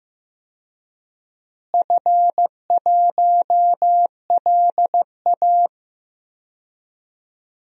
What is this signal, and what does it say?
Morse code, heard as a keyed tone, about 15 wpm, 700 Hz. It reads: F1LA
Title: DXCC-100 Call Signs - 4 Characters - Repeated 3x - 15wpm